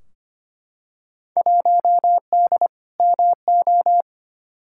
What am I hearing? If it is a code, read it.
Morse code, heard as a keyed tone, about 25 wpm, 700 Hz. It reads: 1D MO
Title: Field Day Exchanges - 25wpm